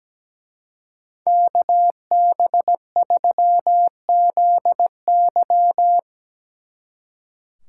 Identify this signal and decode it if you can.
Morse code, heard as a keyed tone, about 18 wpm, 700 Hz. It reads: KB3ZY
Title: Speed Racing - USA 2x2 Call Signs 17wpm